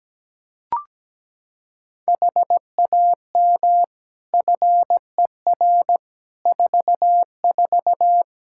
Morse code, heard as a keyed tone, about 17 wpm, 700 Hz. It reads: HAM FER 44
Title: Practice QSO - Exchange 3 - More Info - 17wpm